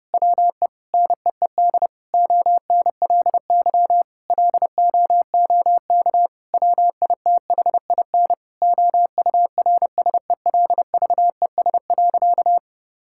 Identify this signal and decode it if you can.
Morse code, heard as a keyed tone, about 31 wpm, 700 Hz. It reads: WE NEED ONLY LOOK WITHIN OURSELVES.
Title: Sentences from Top 1600 Words 30wpm